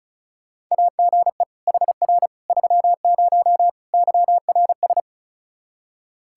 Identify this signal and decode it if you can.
Morse code, heard as a keyed tone, about 36 wpm, 700 Hz. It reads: AGE HR 30 YRS
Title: Practice QSO - Exchange 3 - More Info - 35wpm